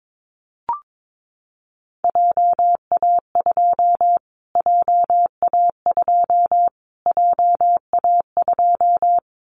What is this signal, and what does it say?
Morse code, heard as a keyed tone, about 22 wpm, 700 Hz. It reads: JA2 JA2 JA2
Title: Callsign Prefixes for Top 25 DXCC - Repeated 3x - 22wpm